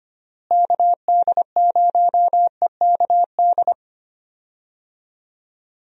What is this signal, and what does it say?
Morse code, heard as a keyed tone, about 25 wpm, 700 Hz. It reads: KD0EKD
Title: USA Call Signs - 6 Characters - Repeated 3x - 25wpm